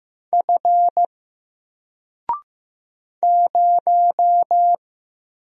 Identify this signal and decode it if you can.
Morse code, heard as a keyed tone, about 15 wpm, 700 Hz. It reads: F 0
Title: Single Letter-Number 15wpm